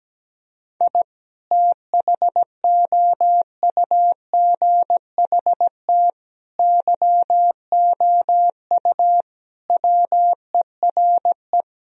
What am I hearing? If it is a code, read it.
Morse code, heard as a keyed tone, about 17 wpm, 700 Hz. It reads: I THOUGHT YOU WERE
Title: Sentences from Top 1000 Words - Review Part 4 - 17wpm